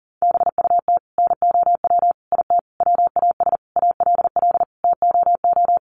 Morse code, heard as a keyed tone, about 40 wpm, 700 Hz. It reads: BUT NOW IT WAS ALL TOO
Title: Sentences from Top 700 Words 40wpm